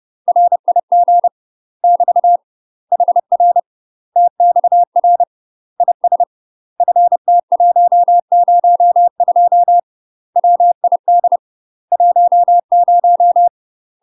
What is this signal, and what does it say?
Morse code, heard as a keyed tone, about 30 wpm, 700 Hz. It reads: RIG <BT> HR TXR IS FT102 WID 10
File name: UK MASTER QSOs TRAINING / 30WPM / UK MASTER QSOs 0013